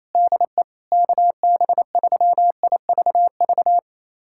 Morse code, heard as a keyed tone, about 28 wpm, 700 Hz. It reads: DE KB3IVV
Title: USA Callsign DE Callsign - 6 Characters - 28wpm